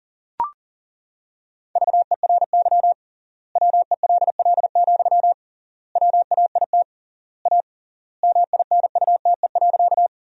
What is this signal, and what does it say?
Morse code, heard as a keyed tone, about 37 wpm, 700 Hz. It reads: VERY WELL, WAIT A MINUTE.
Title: 3x Word-Spacing - Sentences from Top 500 Words - 40wpm